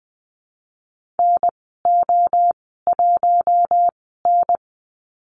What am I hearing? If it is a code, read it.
Morse code, heard as a keyed tone, about 20 wpm, spaced as for 15 wpm, 700 Hz. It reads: NO1N
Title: 055 Lesson 10g - Letter r - Call Signs 20wpm